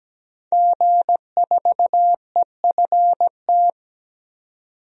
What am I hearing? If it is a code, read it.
Morse code, heard as a keyed tone, about 17 wpm, 700 Hz. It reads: G4EFT
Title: DXCC-100 Call Signs - 5 Characters - 17wpm